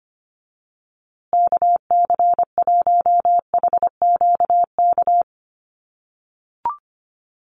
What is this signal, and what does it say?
Morse code, heard as a keyed tone, about 25 wpm, 700 Hz. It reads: KC1HQK
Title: USA Call Signs Repeated 3x 25wpm